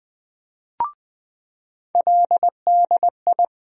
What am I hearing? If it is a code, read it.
Morse code, heard as a keyed tone, about 20 wpm, 700 Hz. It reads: LDI
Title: USA Call Sign Postfixes - 3 Letters 20wpm